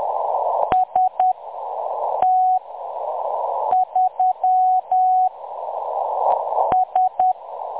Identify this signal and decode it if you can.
Morse code, heard as a keyed tone, about 10 wpm, spaced as for 6 wpm, 740 Hz. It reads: ST3 S
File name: M23-ST3- -0858utc-10184khz